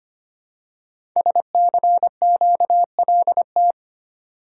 Morse code, heard as a keyed tone, about 25 wpm, 700 Hz. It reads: SCQLT
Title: Code Groups - 5 Letters - 25wpm